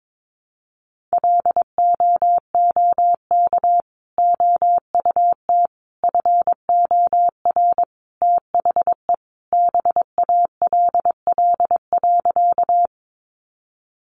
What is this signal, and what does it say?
Morse code, heard as a keyed tone, about 22 wpm, 700 Hz. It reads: LOOK OUT FOR THE BALL.
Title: Sentences from Top 900 Words 22wpm